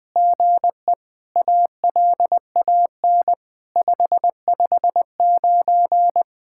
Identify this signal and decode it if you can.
Morse code, heard as a keyed tone, about 20 wpm, 700 Hz. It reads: GE ALAN 559
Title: SOTA QSO Practice 20wpm